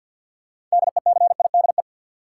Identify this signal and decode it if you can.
Morse code, heard as a keyed tone, about 47 wpm, 700 Hz. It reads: DECIDE
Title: Sentences from Top 500 Words - New Word Drill - 50wpm